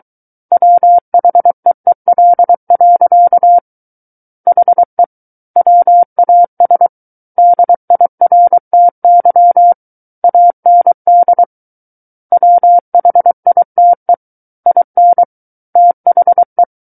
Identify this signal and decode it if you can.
Morse code, heard as a keyed tone, about 23 wpm, 700 Hz. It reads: WHEEL. HE WAS DIRTY AND WHITE IN THE
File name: Book One / 14-Chapter 14 - 23 WPM